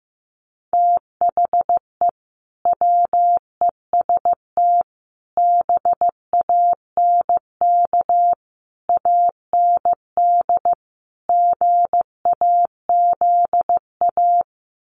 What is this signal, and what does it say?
Morse code, heard as a keyed tone, about 15 wpm, 700 Hz. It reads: THE WEST BANK AND GAZA
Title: Sets of 5 Words 15wpm